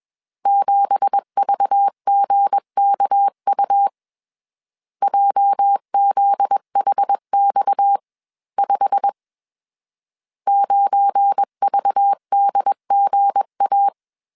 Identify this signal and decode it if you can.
Morse code, heard as a keyed tone, about 21 wpm, 800 Hz. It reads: 7VGKU JZHX5 9VDGA